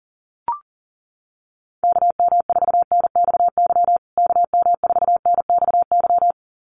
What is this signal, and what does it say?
Morse code, heard as a keyed tone, about 40 wpm, 700 Hz. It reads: KM4NXY KM4NXY
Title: USA Call Signs Repeated 2x 40wpm